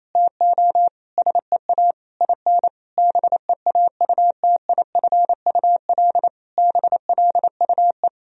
Morse code, heard as a keyed tone, about 28 wpm, 700 Hz. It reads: TO SEA IN BEAUTIFUL BLUE
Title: Sentences from Top 1000 Words - Review Part 3 - 28wpm